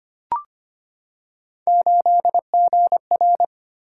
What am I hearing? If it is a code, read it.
Morse code, heard as a keyed tone, about 25 wpm, 700 Hz. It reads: 8GR
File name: Three Letter-Number Combinations - 25wpm